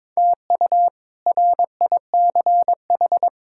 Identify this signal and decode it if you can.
Morse code, heard as a keyed tone, about 22 wpm, 700 Hz. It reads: TU RICH
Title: K1USN SST QSO Practice - 22wpm